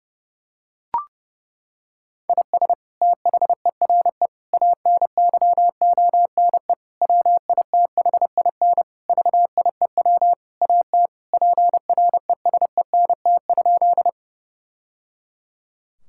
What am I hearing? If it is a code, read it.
Morse code, heard as a keyed tone, about 30 wpm, 700 Hz. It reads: IS THERE ANYONE WITHIN VIEW AT PRESENT?
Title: Sentences from Top 1000 Words - Review Part 6 - 30wpm